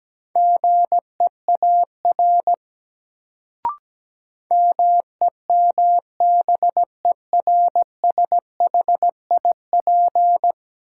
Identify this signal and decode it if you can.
Morse code, heard as a keyed tone, about 17 wpm, 700 Hz. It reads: GEAR MEMBERSHIP
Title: Sentences from Top 1400 Words - New Word Drill Alt - 17wpm